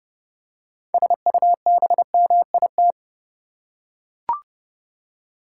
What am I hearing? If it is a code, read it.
Morse code, heard as a keyed tone, about 30 wpm, 700 Hz. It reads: SUBMIT
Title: Six Letter Words - 30wpm